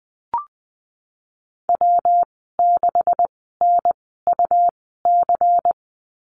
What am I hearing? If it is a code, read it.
Morse code, heard as a keyed tone, about 20 wpm, spaced as for 15 wpm, 700 Hz. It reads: W6NUC
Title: 173 Lesson 28g - Number 9 - Call Signs 20wpm